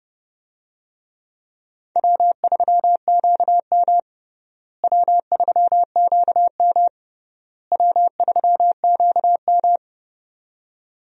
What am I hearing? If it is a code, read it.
Morse code, heard as a keyed tone, about 32 wpm, 700 Hz. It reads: W3QM W3QM W3QM
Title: USA Call Signs - 4 Characters - Repeated 3x - 30wpm